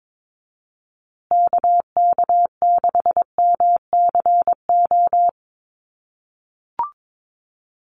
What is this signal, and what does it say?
Morse code, heard as a keyed tone, about 22 wpm, 700 Hz. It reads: KK6MCO
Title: USA Call Signs Repeated 3x 22wpm